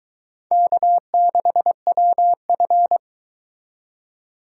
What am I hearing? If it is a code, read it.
Morse code, heard as a keyed tone, about 22 wpm, 700 Hz. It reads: K6WF